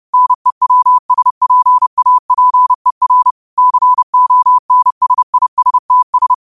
Morse code, heard as a keyed tone, about 30 wpm, 1000 Hz. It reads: NEWSPAPER CONSISTS
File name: qotdfiles / 30wpm